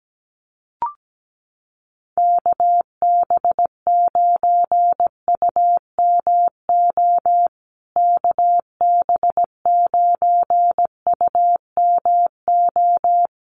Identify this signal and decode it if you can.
Morse code, heard as a keyed tone, about 17 wpm, 700 Hz. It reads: KB9UMO KB9UMO
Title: USA Call Signs Repeated 2x 17wpm